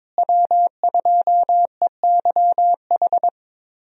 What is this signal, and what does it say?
Morse code, heard as a keyed tone, about 22 wpm, 700 Hz. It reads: W2EYH
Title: USA Call Signs - 5 Characters - Repeated 2x - 22wpm